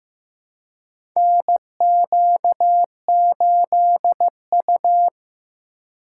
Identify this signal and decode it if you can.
Morse code, heard as a keyed tone, about 15 wpm, 700 Hz. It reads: NQ8U
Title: USA Call Signs - 4 Characters - 8x Character Spacing - 15wpm